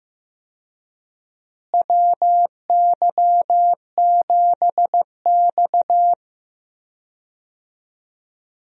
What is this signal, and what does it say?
Morse code, heard as a keyed tone, about 15 wpm, 700 Hz. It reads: WY7X
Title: USA Call Signs - 4 Characters - Repeated 3x - 15wpm